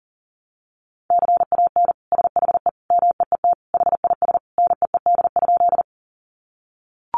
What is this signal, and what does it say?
Morse code, heard as a keyed tone, about 40 wpm, 700 Hz. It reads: CAN SHE MEET HIS NEED?
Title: Sentences from Top 1100 Words 40wpm